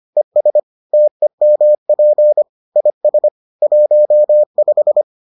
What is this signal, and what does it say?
Morse code, heard as a keyed tone, about 25 wpm, 600 Hz. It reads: ES TEMP IS 15